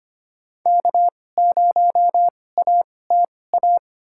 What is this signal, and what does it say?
Morse code, heard as a keyed tone, about 25 wpm, spaced as for 18 wpm, 700 Hz. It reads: K0ATA
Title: USA Call Signs - 5 Characters - 2x Character Spacing - 25wpm